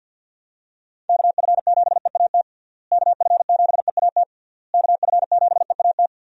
Morse code, heard as a keyed tone, about 50 wpm, 700 Hz. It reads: KF7EAT KF7EAT KF7EAT
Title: USA Call Signs - 6 Characters - Repeated 3x - 50wpm